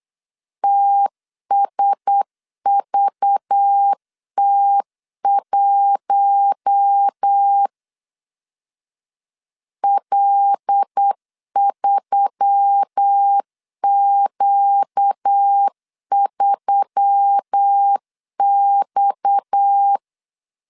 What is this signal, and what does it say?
Morse code, heard as a keyed tone, about 8 wpm, 800 Hz. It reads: TSVT1 L3Q3X
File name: CW / DET / 1 6 1500C S B 8wpm 5142